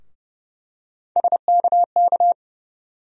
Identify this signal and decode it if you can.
Morse code, heard as a keyed tone, about 32 wpm, 700 Hz. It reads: SKK
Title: USA Call Sign Postfixes - 3 Letters 30wpm